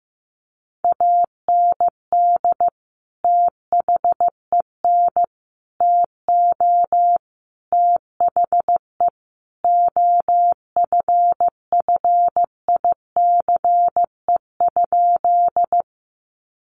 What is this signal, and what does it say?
Morse code, heard as a keyed tone, about 15 wpm, 700 Hz. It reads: AND THEN TO THE OFFICE?
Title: Sentences from Top 1000 Words 15wpm